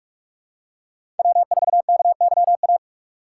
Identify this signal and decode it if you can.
Morse code, heard as a keyed tone, about 46 wpm, 700 Hz. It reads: W4KYA
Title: USA Call Signs - 5 Characters - 2x Character Spacing - 45wpm